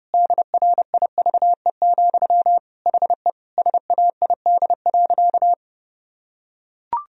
Morse code, heard as a keyed tone, about 30 wpm, 700 Hz. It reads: DRIVE, HE SAID.
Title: Sentences from Top 1000 Words - Review Part 2 - 30wpm